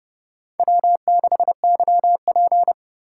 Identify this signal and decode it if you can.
Morse code, heard as a keyed tone, about 28 wpm, 700 Hz. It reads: W6YP